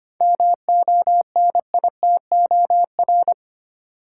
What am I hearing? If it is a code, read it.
Morse code, heard as a keyed tone, about 25 wpm, 700 Hz. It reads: MONITOR
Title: Seven Letter Words - 25wpm